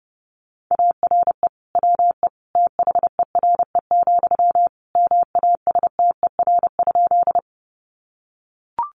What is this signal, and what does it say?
Morse code, heard as a keyed tone, about 30 wpm, 700 Hz. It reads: ARE WE THERE, MASTER?
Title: Sentences from Top 1300 Words 30wpm